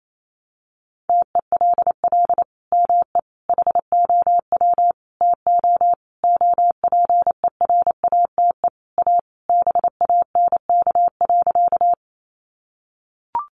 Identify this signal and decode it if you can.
Morse code, heard as a keyed tone, about 28 wpm, 700 Hz. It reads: TELL ME HOW TO OPERATE A BANK.
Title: Sentences from Top 1500 Words 28wpm